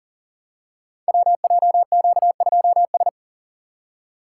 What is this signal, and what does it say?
Morse code, heard as a keyed tone, about 39 wpm, 700 Hz. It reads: WJQ2S
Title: Code Groups - 5 Letter:Numbers - 40wpm